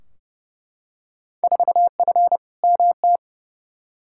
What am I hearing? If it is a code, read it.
Morse code, heard as a keyed tone, about 31 wpm, 700 Hz. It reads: 4F MT
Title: Field Day Exchanges Encore - 30wpm